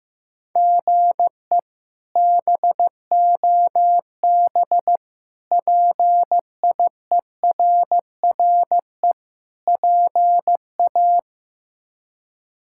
Morse code, heard as a keyed tone, about 15 wpm, 700 Hz. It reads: GE BOB PIERRE PA
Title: K1USN SST QSO Practice - 15wpm